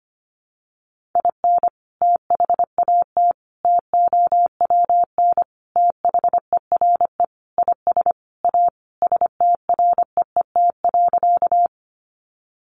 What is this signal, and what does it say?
Morse code, heard as a keyed tone, about 25 wpm, 700 Hz. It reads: IN THAT TOWN THERE IS A STREET.
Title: Sentences from Top 500 Words 25wpm